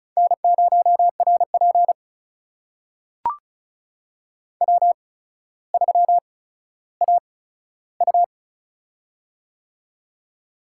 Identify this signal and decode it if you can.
Morse code, heard as a keyed tone, about 36 wpm, spaced as for 15 wpm, 700 Hz. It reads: N0RP W3AU
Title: USA Call Signs - 4 Characters - 8x Character Spacing - 35wpm